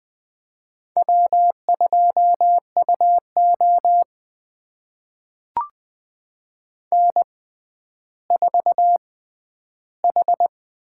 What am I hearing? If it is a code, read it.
Morse code, heard as a keyed tone, about 20 wpm, spaced as for 11 wpm, 700 Hz. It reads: W2UO N4H
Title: USA Call Signs - 6x Character Spacing - 20wpm